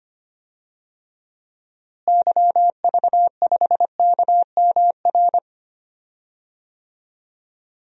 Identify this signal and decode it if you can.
Morse code, heard as a keyed tone, about 25 wpm, 700 Hz. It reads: YV5KMR
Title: DXCC-100 Call Signs - 6 Characters - Repeated 3x - 25wpm